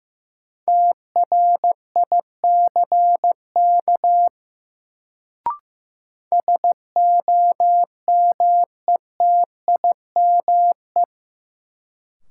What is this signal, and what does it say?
Morse code, heard as a keyed tone, about 16 wpm, 700 Hz. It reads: TRICK SOMETIME
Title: Sentences from Top 1500 Words - New Word Drill Alt - 15wpm